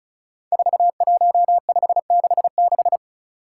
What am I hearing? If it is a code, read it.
Morse code, heard as a keyed tone, about 36 wpm, 700 Hz. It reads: 41566
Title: Five Challenging Letter-Number Combinations 35wpm